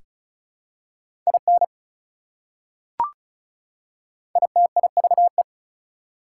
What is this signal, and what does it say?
Morse code, heard as a keyed tone, about 36 wpm, 700 Hz. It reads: IN ITIVE
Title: Prefixes and Suffixes 35wpm